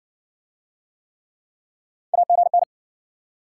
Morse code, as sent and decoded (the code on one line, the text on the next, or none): .- --. -.
AGN